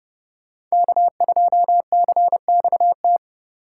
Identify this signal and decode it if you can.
Morse code, heard as a keyed tone, about 33 wpm, 700 Hz. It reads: K2CXT